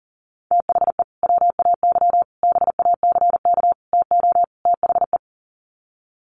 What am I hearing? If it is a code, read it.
Morse code, heard as a keyed tone, about 40 wpm, 700 Hz. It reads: THE WAY BACK TO THE
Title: Sets of 5 Words Encore2 40wpm